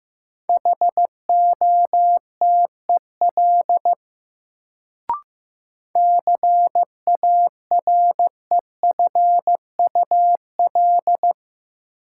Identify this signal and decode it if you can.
Morse code, heard as a keyed tone, about 15 wpm, 700 Hz. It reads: HOTEL CAREFUL
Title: Sentences from Top 1600 Words - New Word Drill - 15wpm